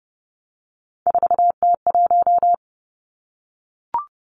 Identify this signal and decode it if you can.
Morse code, heard as a keyed tone, about 30 wpm, 700 Hz. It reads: 4T1
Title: Three Challenging Letter-Number Combinations 30wpm